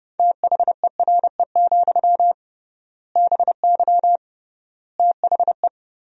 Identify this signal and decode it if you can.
Morse code, heard as a keyed tone, about 30 wpm, 700 Hz. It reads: THERE, BY THE
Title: 3x Word-Spacing - Sentences from Top 500 Words - 30wpm